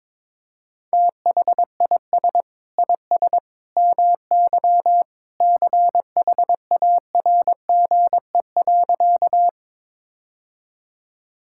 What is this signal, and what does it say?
Morse code, heard as a keyed tone, about 22 wpm, 700 Hz. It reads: THIS IS MY CHARGE.
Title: Sentences from Top 1000 Words - Review Part 6 - 22wpm